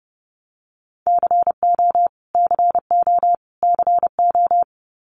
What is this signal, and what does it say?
Morse code, heard as a keyed tone, about 30 wpm, 700 Hz. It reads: CO CO CO
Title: US States Repeated 3x 30wpm